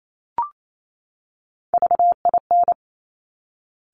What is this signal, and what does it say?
Morse code, heard as a keyed tone, about 28 wpm, 700 Hz. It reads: VIN
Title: Top Three Letter Combinations in Words 28wpm